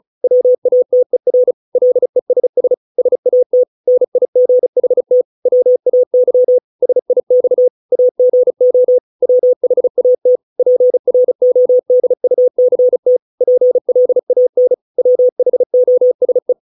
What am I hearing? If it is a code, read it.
Morse code, heard as a keyed tone, about 34 wpm, 500 Hz. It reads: WATER LESS SAT NIGHT WAY SIX AGO WHAT PRODUCT PLAN WHOSE
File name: 632 words / 632 Words 35WPM